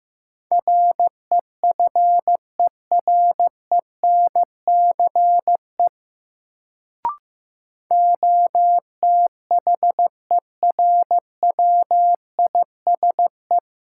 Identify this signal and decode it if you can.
Morse code, heard as a keyed tone, about 15 wpm, 700 Hz. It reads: REFERENCE OTHERWISE
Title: Sentences from Top 1200 Words - New Word Drill Alt - 15wpm